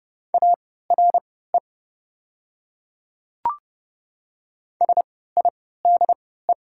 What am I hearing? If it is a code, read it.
Morse code, heard as a keyed tone, about 31 wpm, spaced as for 17 wpm, 700 Hz. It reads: ARE SIDE
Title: 3x Character Spacing - Top 100 Words - 30wpm